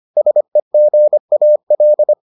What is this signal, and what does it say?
Morse code, heard as a keyed tone, about 25 wpm, 600 Hz. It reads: SEGAL